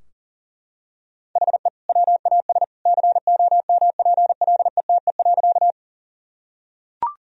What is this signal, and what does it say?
Morse code, heard as a keyed tone, about 40 wpm, 700 Hz. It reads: HE WAS COMPLETE.
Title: Sentences from Top 1000 Words - Review Part 2 - 40wpm